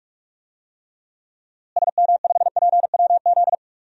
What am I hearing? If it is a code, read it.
Morse code, heard as a keyed tone, about 41 wpm, 700 Hz. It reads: SM5PWZ
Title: DXCC-100 Call Signs - 6 Characters - Repeated 3x - 45wpm